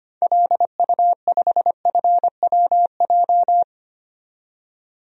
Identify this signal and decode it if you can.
Morse code, heard as a keyed tone, about 25 wpm, 700 Hz. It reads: LU5FWJ
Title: DXCC-100 Call Signs - 6 Characters - 25wpm